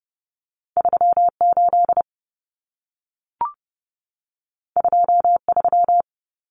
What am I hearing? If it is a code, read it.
Morse code, heard as a keyed tone, about 30 wpm, 700 Hz. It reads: 38 23
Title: Common Cut Numbers - 2-Digit 30wpm